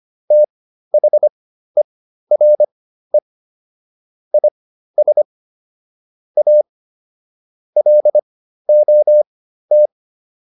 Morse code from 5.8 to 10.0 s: A LOT

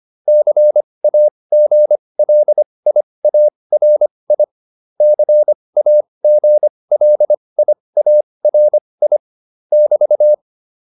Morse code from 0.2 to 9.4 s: CAGLIARI CAGLIARI

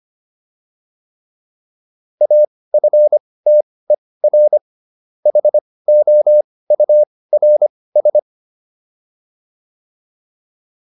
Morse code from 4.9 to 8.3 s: HOURS

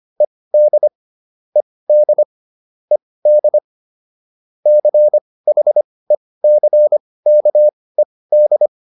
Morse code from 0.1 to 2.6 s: ED ED

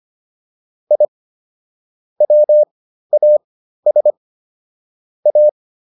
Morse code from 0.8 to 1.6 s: I

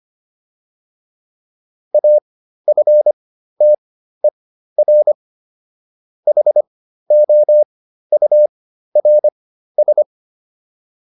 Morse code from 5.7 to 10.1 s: HOURS

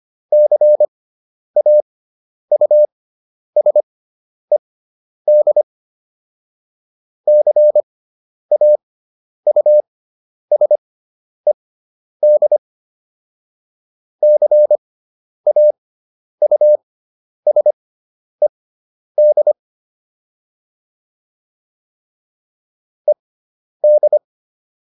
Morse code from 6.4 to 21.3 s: CAUSED CAUSED

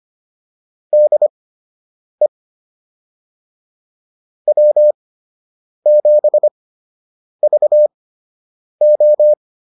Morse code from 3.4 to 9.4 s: W7VO

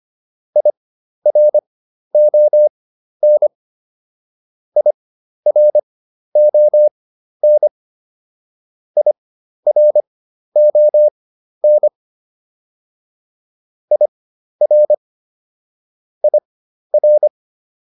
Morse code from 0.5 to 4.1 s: IRON

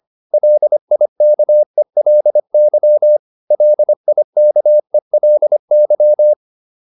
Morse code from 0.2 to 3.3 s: LIKELY

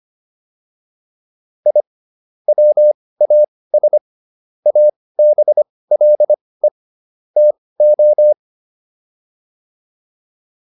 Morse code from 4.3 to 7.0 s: ABLE

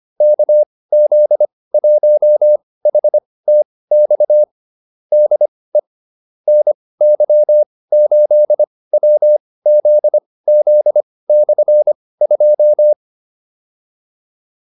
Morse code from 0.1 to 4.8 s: KZ1HTX

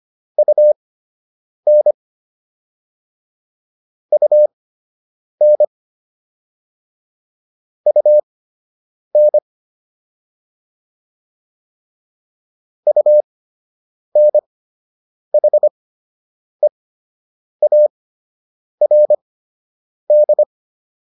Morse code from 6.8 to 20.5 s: UN UNHEARD

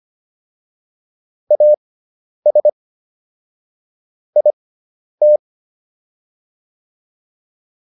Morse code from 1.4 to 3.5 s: AS

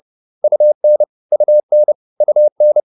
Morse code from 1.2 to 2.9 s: UN UN

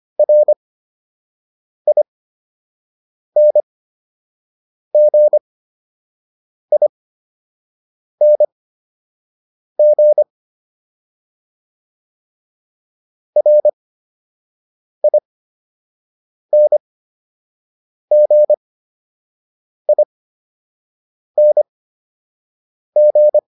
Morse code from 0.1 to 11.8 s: RINGING

